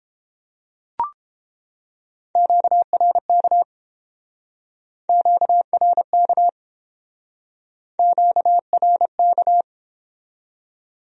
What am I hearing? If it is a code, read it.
Morse code, heard as a keyed tone, about 29 wpm, 700 Hz. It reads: QRK QRK QRK